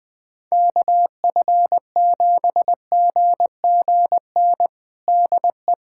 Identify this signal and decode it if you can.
Morse code, heard as a keyed tone, about 21 wpm, 700 Hz. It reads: KF7GGN DE
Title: USA Callsign DE Callsign 20wpm